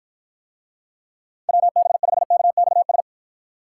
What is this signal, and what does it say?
Morse code, heard as a keyed tone, about 53 wpm, 700 Hz. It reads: WB5KXS